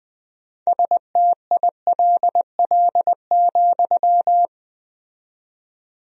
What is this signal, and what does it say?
Morse code, heard as a keyed tone, about 21 wpm, 700 Hz. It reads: STILL,
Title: 4x Word-Spacing - Sentences from Top 300 Words - 20wpm